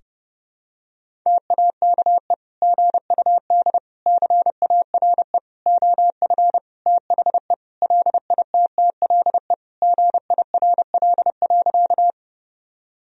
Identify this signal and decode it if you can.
Morse code, heard as a keyed tone, about 30 wpm, 700 Hz. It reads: TAKE GUD CARE OF THE LITTLE GIRL.
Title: Sentences from Top 400 Words and Abbreviations 30wpm